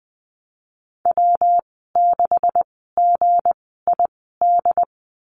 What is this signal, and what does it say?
Morse code, heard as a keyed tone, about 20 wpm, spaced as for 15 wpm, 700 Hz. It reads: W6GID
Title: 173 Lesson 28g - Number 9 - Call Signs 20wpm